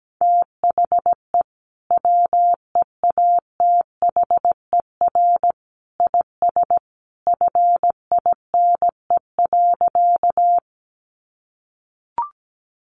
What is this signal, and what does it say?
Morse code, heard as a keyed tone, about 17 wpm, 700 Hz. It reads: THE WEATHER IS FINE.
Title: Sentences from Top 800 Words 17wpm